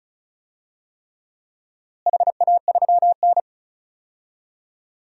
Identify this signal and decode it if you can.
Morse code, heard as a keyed tone, about 36 wpm, 700 Hz. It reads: HA3N
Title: DXCC-100 Call Signs - 4 Characters - Repeated 3x - 35wpm